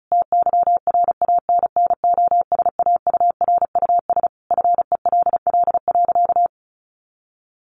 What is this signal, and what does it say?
Morse code, heard as a keyed tone, about 34 wpm, 700 Hz. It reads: TYRANNOSAURUS FELL.
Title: A Sound Of Thunder 35wpm